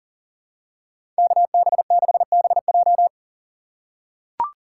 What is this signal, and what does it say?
Morse code, heard as a keyed tone, about 40 wpm, 700 Hz. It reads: KB6BJ
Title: USA Call Signs - 5 Characters - 6x Character Spacing - 40wpm